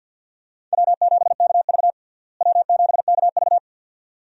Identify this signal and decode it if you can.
Morse code, heard as a keyed tone, about 52 wpm, 700 Hz. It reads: W7KV W7KV
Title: USA Call Signs - 4 Characters - Repeated 2x - 50wpm